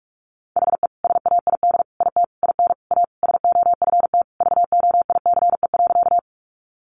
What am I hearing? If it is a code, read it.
Morse code, heard as a keyed tone, about 44 wpm, 700 Hz. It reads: HE SAID IT IN A SOFT VOICE.
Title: Sentences from Top 1500 Words 45wpm